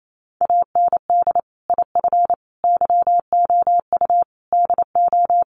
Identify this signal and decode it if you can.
Morse code, heard as a keyed tone, about 28 wpm, 700 Hz. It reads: AND IF YOU DO
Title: Sets of 5 Words 28wpm